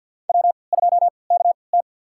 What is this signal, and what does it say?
Morse code, heard as a keyed tone, about 53 wpm, spaced as for 29 wpm, 700 Hz. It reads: W2KT
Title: USA Call Signs - 4 Characters - 3x Character Spacing - 50wpm